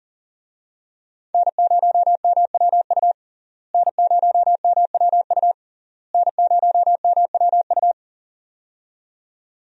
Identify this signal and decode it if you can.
Morse code, heard as a keyed tone, about 40 wpm, 700 Hz. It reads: N0MWU N0MWU N0MWU
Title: USA Call Signs - 5 Characters - Repeated 3x - 40wpm